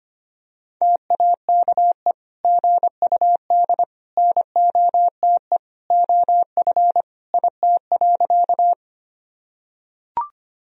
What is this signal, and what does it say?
Morse code, heard as a keyed tone, about 25 wpm, 700 Hz. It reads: TAKE GUD NOTE OF IT.
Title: Sentences from Top 500 Words and Abbreviations 25wpm